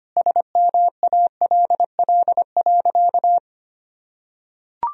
Morse code, heard as a keyed tone, about 25 wpm, 700 Hz. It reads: SMALL.
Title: Sentences from Top 1000 Words - Review Part 1 - 25wpm